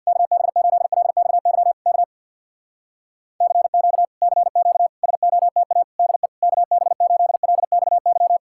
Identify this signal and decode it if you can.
Morse code, heard as a keyed tone, about 66 wpm, 700 Hz. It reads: KB8LXY K CQ CQ SOTA DE KB8LXY